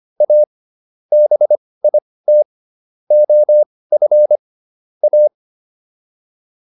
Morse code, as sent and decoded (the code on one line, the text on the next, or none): .- / -... .. - / --- ..-. / .-
A BIT OF A